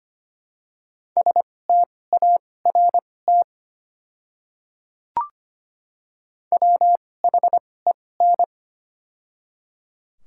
Morse code, heard as a keyed tone, about 25 wpm, spaced as for 18 wpm, 700 Hz. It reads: START WHEN
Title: 2x Character Spacing - Top 500 Words - 25wpm